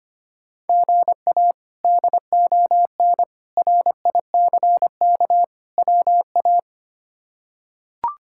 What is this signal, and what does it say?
Morse code, heard as a keyed tone, about 25 wpm, 700 Hz. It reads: GA DON RICK WA
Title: K1USN SST QSO Practice - 25wpm